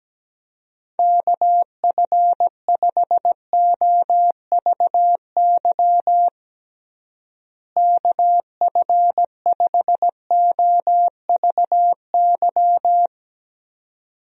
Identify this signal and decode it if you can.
Morse code, heard as a keyed tone, about 17 wpm, 700 Hz. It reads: KF5OVY KF5OVY
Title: USA Call Signs - 6 Characters - Repeated 2x - 17wpm